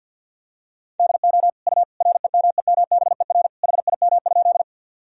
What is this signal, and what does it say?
Morse code, heard as a keyed tone, about 47 wpm, 700 Hz. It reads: DO U REMEMBER HIM?
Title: Sentences from Top 400 Words and Abbreviations 50wpm